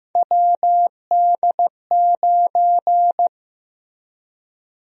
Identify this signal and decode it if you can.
Morse code, heard as a keyed tone, about 15 wpm, 700 Hz. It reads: WD9
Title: USA Call Sign Prefixes - 15wpm